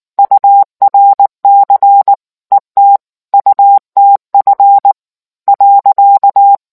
Morse code, heard as a keyed tone, about 19 wpm, 800 Hz. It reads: URC ETUTF .